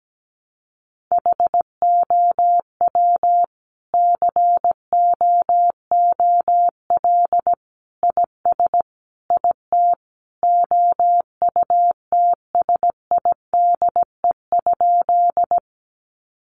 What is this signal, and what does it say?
Morse code, heard as a keyed tone, about 17 wpm, 700 Hz. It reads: HOW COOL IS IT OUTSIDE?
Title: Sentences from Top 1200 Words 17wpm